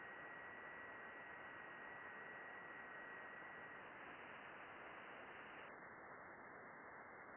FT8.
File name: contests / june-vhf-k5tr- -0200